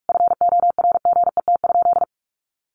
Morse code, heard as a keyed tone, about 44 wpm, 700 Hz. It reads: FORGET?
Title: Sentences from Top 1300 Words 45wpm